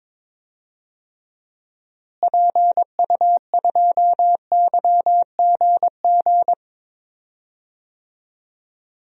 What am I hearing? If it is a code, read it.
Morse code, heard as a keyed tone, about 22 wpm, 700 Hz. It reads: PU2YGG
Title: DXCC-100 Call Signs - 6 Characters - Repeated 3x - 22wpm